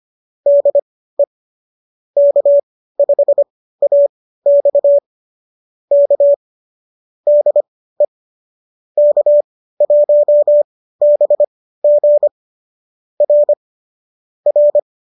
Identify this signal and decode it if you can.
Morse code, heard as a keyed tone, about 25 wpm, spaced as for 15 wpm, 570 Hz. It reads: DE K5AX K DE K1BG R R
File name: qso201 15